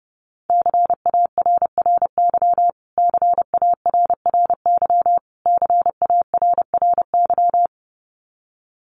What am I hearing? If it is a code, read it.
Morse code, heard as a keyed tone, about 28 wpm, 700 Hz. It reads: CARRY CARRY CARRY
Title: Top 500 Words Repeated 30wpm